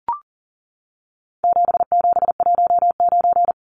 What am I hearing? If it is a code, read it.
Morse code, heard as a keyed tone, about 40 wpm, 700 Hz. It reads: 7719
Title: Numbers - 4-Digit 40wpm